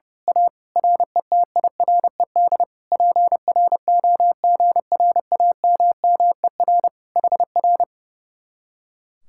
Morse code, heard as a keyed tone, about 29 wpm, 700 Hz. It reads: A RETIRED PROGRAMMER HR
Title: Speed-Racing Common QSO Phrases - 30wpm